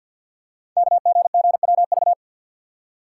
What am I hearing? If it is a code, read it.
Morse code, heard as a keyed tone, about 50 wpm, 690 Hz. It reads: KGGWV
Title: Code Groups - 5 Letters - 50wpm